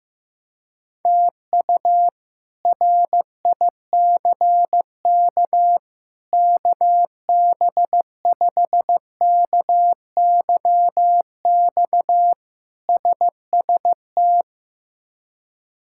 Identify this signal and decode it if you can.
Morse code, heard as a keyed tone, about 16 wpm, 700 Hz. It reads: TU RICK KB5KYX SST
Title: K1USN SST QSO Practice - 15wpm